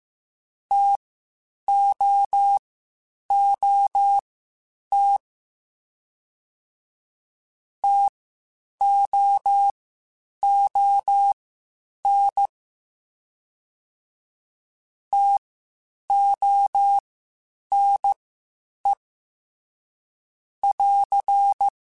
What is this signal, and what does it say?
Morse code, heard as a keyed tone, about 15 wpm, spaced as for 8 wpm, 800 Hz. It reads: TOOT TOON TONE <AR>
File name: Spaced Alphas / Spaced Alphas mp3s / Ts sp @13